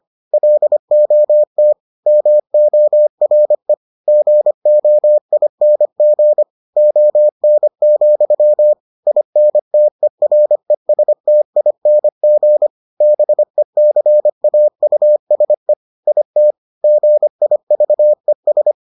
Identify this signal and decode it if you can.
Morse code, heard as a keyed tone, about 25 wpm, 600 Hz. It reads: LOT MORE GOING ON, INTERESTING BECAUSE IT GIVES